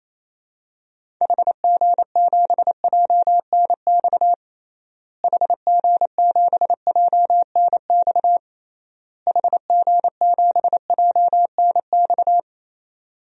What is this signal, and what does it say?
Morse code, heard as a keyed tone, about 28 wpm, 700 Hz. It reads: HG7JNX HG7JNX HG7JNX
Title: DXCC-100 Call Signs - 6 Characters - Repeated 3x - 28wpm